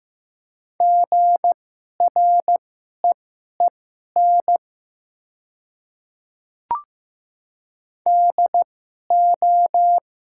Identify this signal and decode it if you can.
Morse code, heard as a keyed tone, about 15 wpm, spaced as for 11 wpm, 700 Hz. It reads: GREEN DO
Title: 2x Character Spacing - Top 500 Words - 15wpm